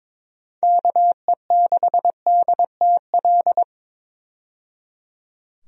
Morse code, heard as a keyed tone, about 22 wpm, 700 Hz. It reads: KE6DTL
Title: USA Call Signs - 6 Characters - 22wpm